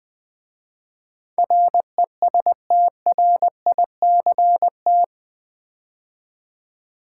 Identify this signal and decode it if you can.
Morse code, heard as a keyed tone, about 20 wpm, 700 Hz. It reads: RESTRICT
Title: 8 Letter Words - First 3 Letters Given - 20wpm